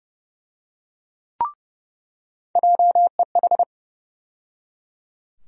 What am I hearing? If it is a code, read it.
Morse code, heard as a keyed tone, about 30 wpm, 700 Hz. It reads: JEH
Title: USA Call Sign Postfixes - 3 Letters 30wpm